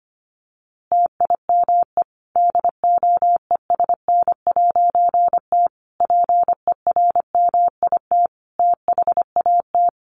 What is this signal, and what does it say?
Morse code, heard as a keyed tone, about 25 wpm, 700 Hz. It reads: TIME DOESN'T PERMIT THAT
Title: A Sound Of Thunder 25wpm